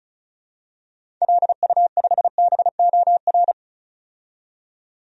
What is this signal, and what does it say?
Morse code, heard as a keyed tone, about 34 wpm, 700 Hz. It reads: LU5BOR
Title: DXCC-100 Call Signs - 6 Characters - 35wpm